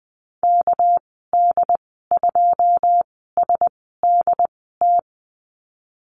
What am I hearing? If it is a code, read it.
Morse code, heard as a keyed tone, about 20 wpm, spaced as for 14 wpm, 700 Hz. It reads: KD2SDT
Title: 200 Lesson 32g - Letter k - Call Signs 20wpm